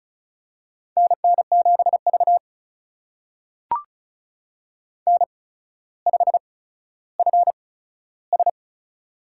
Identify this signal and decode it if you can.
Morse code, heard as a keyed tone, about 36 wpm, spaced as for 15 wpm, 700 Hz. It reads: NN7V N5FS
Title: USA Call Signs - 4 Characters - 8x Character Spacing - 35wpm